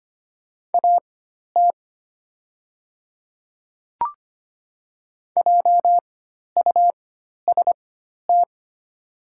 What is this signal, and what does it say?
Morse code, heard as a keyed tone, about 25 wpm, spaced as for 12 wpm, 700 Hz. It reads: AT JUST
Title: 4x Character Spacing - Top 200 Words - 25wpm